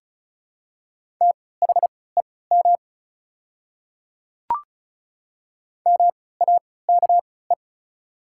Morse code, heard as a keyed tone, about 36 wpm, spaced as for 20 wpm, 700 Hz. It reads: THEM MAKE
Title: 3x Character Spacing - Top 300 Words - 35wpm